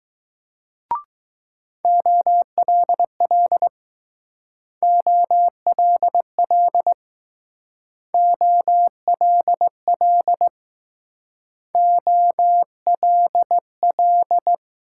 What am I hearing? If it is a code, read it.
Morse code, heard as a keyed tone, about 19 wpm, 700 Hz. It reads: OLL OLL OLL OLL